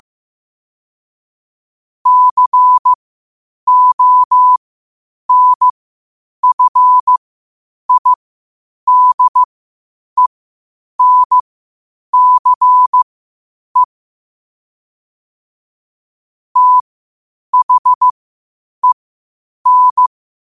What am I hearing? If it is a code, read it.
Morse code, heard as a keyed tone, about 15 wpm, spaced as for 8 wpm, 1000 Hz. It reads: CONFIDENCE THEN
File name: qotdfiles / 8wpm